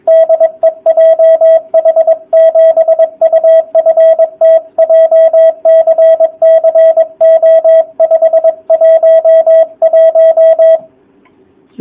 Morse code, heard as a keyed tone, about 22 wpm, 660 Hz, with radio noise. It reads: DE JH7UFT JCC O511
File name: jh8cbh / 2019cw / sp23- w